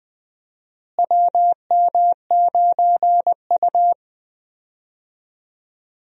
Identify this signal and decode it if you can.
Morse code, heard as a keyed tone, about 20 wpm, 700 Hz. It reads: WM9U
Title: USA Call Signs - 4 Characters - Repeated 2x - 20wpm